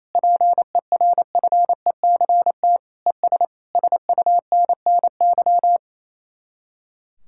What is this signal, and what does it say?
Morse code, heard as a keyed tone, about 36 wpm, 700 Hz. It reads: PERFECT ES SUNNY